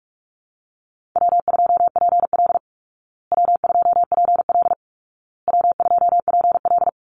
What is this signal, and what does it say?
Morse code, heard as a keyed tone, about 44 wpm, 700 Hz. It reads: W2PL W2PL W2PL
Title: Easy Call Signs Repeated 3x - 45wpm